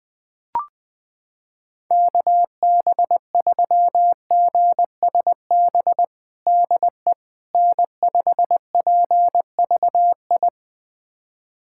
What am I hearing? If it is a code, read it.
Morse code, heard as a keyed tone, about 21 wpm, 700 Hz. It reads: KB3GSB DE N5PVI
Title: USA Callsign DE Callsign 20wpm